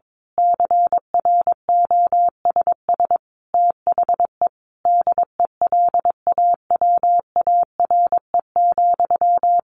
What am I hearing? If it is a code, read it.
Morse code, heard as a keyed tone, about 22 wpm, 700 Hz. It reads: CROSS THE DELAWARE,
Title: A Sound Of Thunder 22wpm